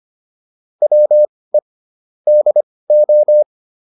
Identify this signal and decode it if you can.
Morse code, heard as a keyed tone, about 25 wpm, spaced as for 18 wpm, 600 Hz. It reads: WE DO